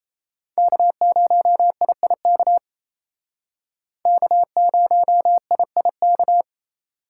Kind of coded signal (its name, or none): Morse code